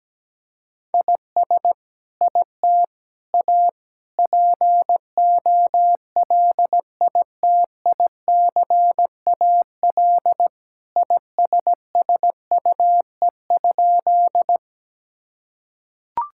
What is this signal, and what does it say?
Morse code, heard as a keyed tone, about 17 wpm, 700 Hz. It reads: IS IT A POLITICAL ISSUE?
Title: Sentences from Top 1000 Words - Review Part 6 - 17wpm